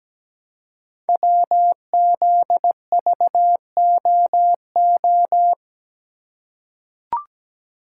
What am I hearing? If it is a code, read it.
Morse code, heard as a keyed tone, about 17 wpm, 700 Hz. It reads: WZVOO
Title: Code Groups - 5 Letter:Numbers - 17wpm